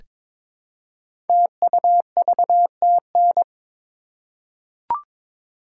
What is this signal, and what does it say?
Morse code, heard as a keyed tone, about 22 wpm, 700 Hz. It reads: TUVTN
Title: Code Groups - 5 Letters - 22wpm